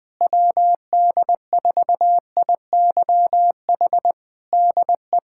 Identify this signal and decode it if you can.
Morse code, heard as a keyed tone, about 20 wpm, 700 Hz. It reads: WD4IYH DE
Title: Practice QSO - Exchange 4 - Ending - 20wpm